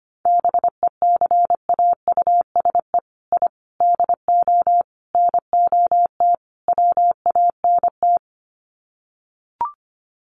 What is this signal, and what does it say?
Morse code, heard as a keyed tone, about 25 wpm, 700 Hz. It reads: BECAUSE I DO NOT WANT
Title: Sets of 5 Words Encore 25wpm